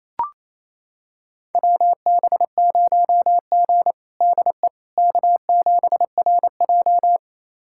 Morse code, heard as a keyed tone, about 28 wpm, 700 Hz. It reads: WB0G DE K7RJ
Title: USA Callsign DE Callsign - 4 Characters - 28wpm